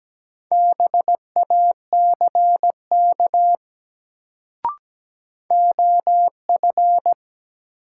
Morse code, heard as a keyed tone, about 17 wpm, 700 Hz. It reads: BACK OF
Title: Sets of 2 Words - New Word Drill - 17wpm